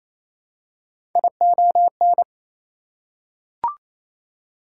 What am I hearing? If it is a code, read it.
Morse code, heard as a keyed tone, about 28 wpm, 700 Hz. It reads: ION
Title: Prefixes and Suffixes 28wpm